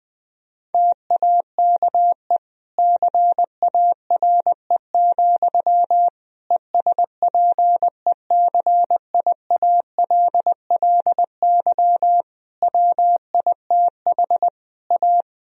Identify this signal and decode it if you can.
Morse code, heard as a keyed tone, about 20 wpm, 700 Hz. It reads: TAKE CARE, ESPECIALLY WITH A
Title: Sentences from Top 1000 Words - Review Part 4 - 20wpm